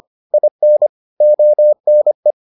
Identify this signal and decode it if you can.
Morse code, heard as a keyed tone, about 25 wpm, 600 Hz. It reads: IN ONE